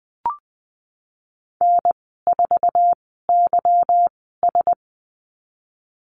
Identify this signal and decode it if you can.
Morse code, heard as a keyed tone, about 20 wpm, spaced as for 15 wpm, 700 Hz. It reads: N4YS
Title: 154 Lesson 25g - Letter p - Call Signs 20wpm